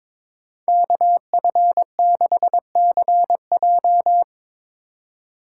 Morse code, heard as a keyed tone, about 22 wpm, 700 Hz. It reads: KF6CJ
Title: USA Call Signs - 5 Characters - 4x Character Spacing - 22wpm